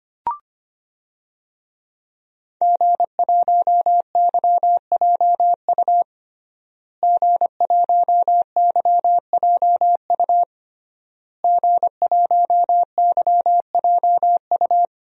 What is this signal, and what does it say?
Morse code, heard as a keyed tone, about 25 wpm, 700 Hz. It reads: G1YJU G1YJU G1YJU
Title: DXCC-100 Call Signs - 5 Characters - Repeated 3x - 25wpm